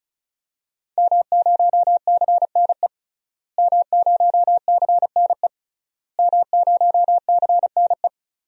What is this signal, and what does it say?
Morse code, heard as a keyed tone, about 36 wpm, 700 Hz. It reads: M0CNE M0CNE M0CNE
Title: DXCC-100 Call Signs - 5 Characters - Repeated 3x - 35wpm